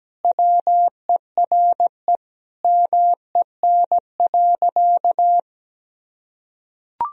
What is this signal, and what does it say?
Morse code, heard as a keyed tone, about 17 wpm, 700 Hz. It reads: WERE MEN.
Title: Sentences from Top 1000 Words - Review Part 1 - 17wpm